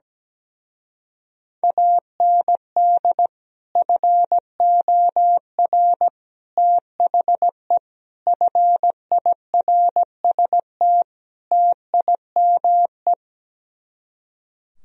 Morse code, heard as a keyed tone, about 17 wpm, 700 Hz. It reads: AND FOR THE FIRST TIME
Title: Speed-Racing - Sets of 5 Words 17wpm